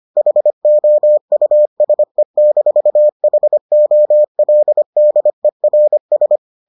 Morse code, heard as a keyed tone, about 25 wpm, 600 Hz. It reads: HOUSE-HOLDERS